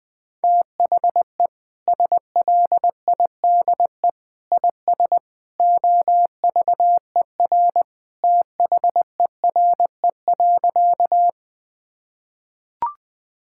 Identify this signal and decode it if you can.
Morse code, heard as a keyed tone, about 19 wpm, 700 Hz. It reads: THE SLIDE IS OVER THERE.
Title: Sentences from Top 1600 Words 20wpm